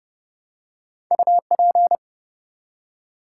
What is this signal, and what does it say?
Morse code, heard as a keyed tone, about 30 wpm, 700 Hz. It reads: UP
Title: Two Letter Words - 30wpm